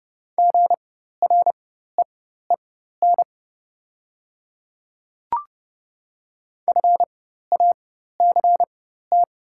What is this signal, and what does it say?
Morse code, heard as a keyed tone, about 30 wpm, spaced as for 14 wpm, 700 Hz. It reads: GREEN FACT
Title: 4x Character Spacing - Top 500 Words - 30wpm